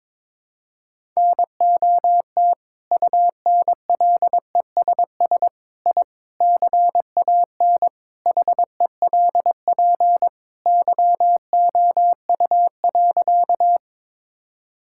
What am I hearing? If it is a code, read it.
Morse code, heard as a keyed tone, about 22 wpm, 700 Hz. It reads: NOT UNLESS I CAN HELP YOU.
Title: Sentences from Top 1000 Words - Review Part 6 - 22wpm